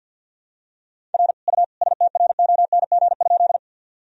Morse code, heard as a keyed tone, about 53 wpm, 700 Hz. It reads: R U STRONG?
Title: Sentences from Top 500 Words and Abbreviations 50wpm